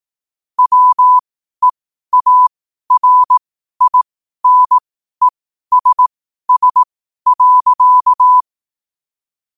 Morse code, heard as a keyed tone, about 18 wpm, spaced as for 12 wpm, 1000 Hz. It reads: WEARINESS.
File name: qotdfiles / 12wpm